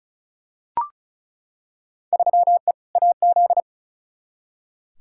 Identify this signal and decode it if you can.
Morse code, heard as a keyed tone, about 36 wpm, 700 Hz. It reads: 3E AZ
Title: Field Day Exchanges Encore - 35wpm